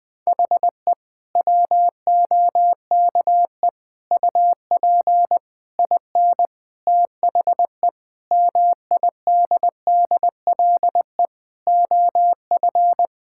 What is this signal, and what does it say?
Morse code, heard as a keyed tone, about 20 wpm, 700 Hz. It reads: HE WOKE UP IN THE MIDDLE OF
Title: Sentences with 7-Word Phrases - Spoken Beginning - 20wpm